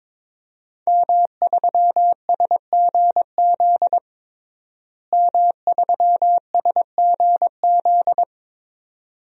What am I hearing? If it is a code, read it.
Morse code, heard as a keyed tone, about 22 wpm, 700 Hz. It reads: M3SGZ M3SGZ
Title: DXCC-100 Call Signs - 5 Characters - Repeated 2x - 22wpm